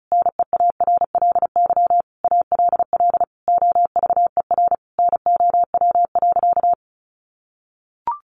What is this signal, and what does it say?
Morse code, heard as a keyed tone, about 34 wpm, 700 Hz. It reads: NEARLY ALL OVER NOW.
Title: Sentences from Top 1300 Words 35wpm